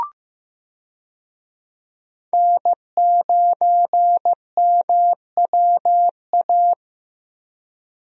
Morse code, heard as a keyed tone, about 16 wpm, 700 Hz. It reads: N9MWA
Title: USA Call Signs - 5 Characters - Repeated 3x - 15wpm